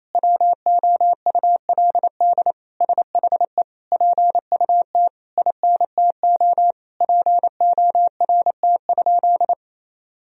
Morse code, heard as a keyed tone, about 28 wpm, 700 Hz. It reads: WOULD SHE PUT INTO PORT?
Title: Sentences from Top 1000 Words - Review Part 1 - 28wpm